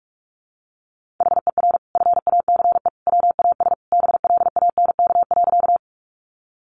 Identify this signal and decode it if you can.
Morse code, heard as a keyed tone, about 44 wpm, 700 Hz. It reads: HER FACE WAS BLANK.
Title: Sentences from Top 1500 Words 45wpm